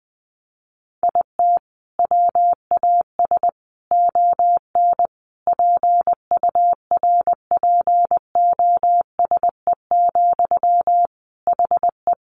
Morse code, heard as a keyed tone, about 20 wpm, 700 Hz. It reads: IT WAS ON PURPOSE, HE
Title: Sentences from Top 800 Words 20wpm